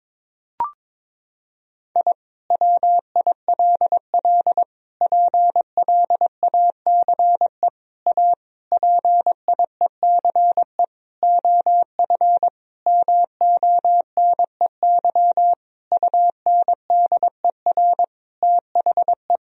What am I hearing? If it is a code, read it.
Morse code, heard as a keyed tone, about 22 wpm, 700 Hz. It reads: I WILL PLACE A PIECE OF MONEY UNDER THE
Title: Sentences from Top 1000 Words - Review Part 2 - 22wpm